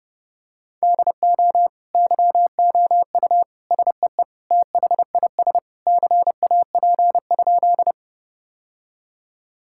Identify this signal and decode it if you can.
Morse code, heard as a keyed tone, about 30 wpm, 700 Hz. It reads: DO YOU SEE THIS CAP?
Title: Sentences from Top 1600 Words 30wpm